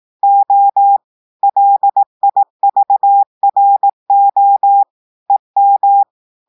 Morse code, heard as a keyed tone, about 18 wpm, 800 Hz. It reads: O LIVRO EM
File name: audios / portugais / portugais texte en clair 1 vitesse 18 mots minutes